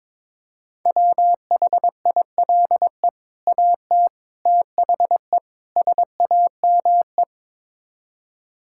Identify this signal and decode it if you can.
Morse code, heard as a keyed tone, about 22 wpm, 700 Hz. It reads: WHILE AT THE SAME
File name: practice / 22wpm / Build A Phrase - 5-Word Phrases - 22wpm